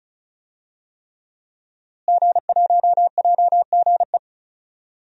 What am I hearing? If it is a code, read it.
Morse code, heard as a keyed tone, about 36 wpm, 700 Hz. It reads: G1JGE
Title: DXCC-100 Call Signs - 5 Characters - Repeated 3x - 35wpm